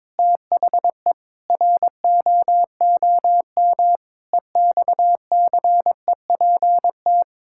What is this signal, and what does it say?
Morse code, heard as a keyed tone, about 22 wpm, 700 Hz. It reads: THE ROOM EXCEPT
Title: Sentences from Top 1600 Words 22wpm